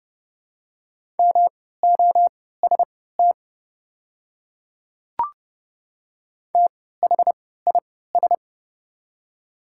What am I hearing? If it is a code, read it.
Morse code, heard as a keyed tone, about 31 wpm, spaced as for 17 wpm, 690 Hz. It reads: MOST THIS
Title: 3x Character Spacing - Top 100 Words - 30wpm